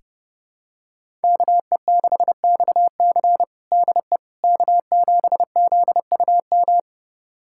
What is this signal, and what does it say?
Morse code, heard as a keyed tone, about 30 wpm, 700 Hz. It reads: KE6XC DE K7ZUM
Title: USA Callsign DE Callsign - 5 Characters - 30wpm